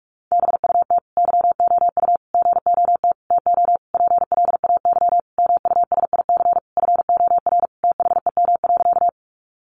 Morse code, heard as a keyed tone, about 45 wpm, 700 Hz. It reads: BUT YOU GOT TO PLAY MUSIC FOR THEM.
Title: Sentences from Top 300 Words 45wpm